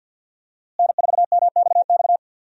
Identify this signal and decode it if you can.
Morse code, heard as a keyed tone, about 47 wpm, 700 Hz. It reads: N4MXX
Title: USA Call Signs - 5 Characters - Repeated 3x - 50wpm